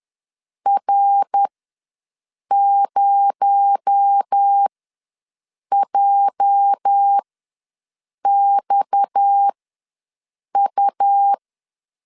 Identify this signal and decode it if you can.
Morse code, heard as a keyed tone, about 10 wpm, spaced as for 6 wpm, 800 Hz. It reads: R0JXU
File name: LEZIONI / 8 YJ45 / 8 YJ ws 10wpm 4043